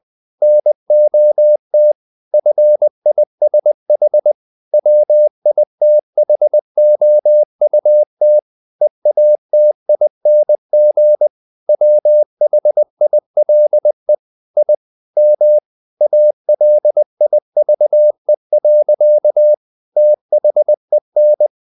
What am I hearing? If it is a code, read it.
Morse code, heard as a keyed tone, about 20 wpm, 600 Hz. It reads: NOT FISH WITHOUT EATING WHILE I M ALIVE. THEN
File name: the old man and the sea / the old man and the sea part 1 20wpm